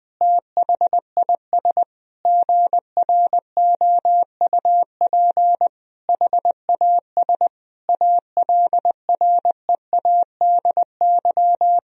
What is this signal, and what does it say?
Morse code, heard as a keyed tone, about 20 wpm, 700 Hz. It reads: THIS GROUP HAS ALREADY
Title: Sentences from Top 1000 Words - Review Part 1 - 20wpm